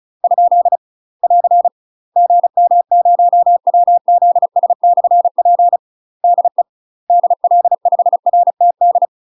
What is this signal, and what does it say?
Morse code, heard as a keyed tone, about 36 wpm, 700 Hz. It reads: ? <AR> GM0WZS/P DE DL5RTD
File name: UK MASTER QSOs TRAINING / 35WPM / UK MASTER QSOs 0035